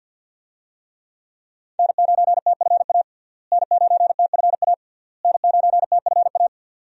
Morse code, heard as a keyed tone, about 47 wpm, 700 Hz. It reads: N9TFA N9TFA N9TFA
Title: USA Call Signs - 5 Characters - Repeated 3x - 50wpm